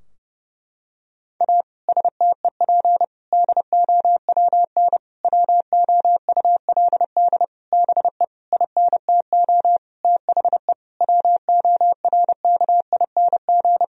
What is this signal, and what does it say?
Morse code, heard as a keyed tone, about 30 wpm, 700 Hz. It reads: A STEP DOWN WOULD BE INTO THE WORKING
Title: Sentences from Top 1000 Words - Review Part 4 - 30wpm